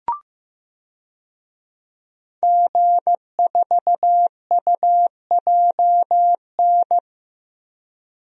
Morse code, heard as a keyed tone, about 15 wpm, 700 Hz. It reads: G4UJN
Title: DXCC-100 Call Signs - 5 Characters - 15wpm